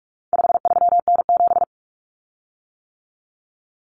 Morse code, heard as a keyed tone, about 44 wpm, 700 Hz. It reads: 53N7
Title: Common Cut Numbers - 4-Digit 45wpm